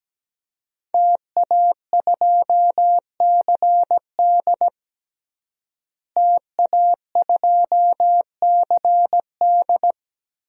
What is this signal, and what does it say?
Morse code, heard as a keyed tone, about 17 wpm, 700 Hz. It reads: TA2CD TA2CD
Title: DXCC-100 Call Signs - 5 Characters - Repeated 2x - 17wpm